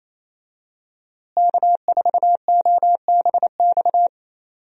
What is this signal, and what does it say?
Morse code, heard as a keyed tone, about 28 wpm, 700 Hz. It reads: K4OBX
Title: USA Call Signs - 5 Characters - Repeated 2x - 28wpm